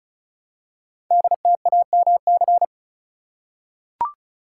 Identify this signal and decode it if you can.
Morse code, heard as a keyed tone, about 36 wpm, 690 Hz. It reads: DTAMC
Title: Code Groups - 5 Letters - 35wpm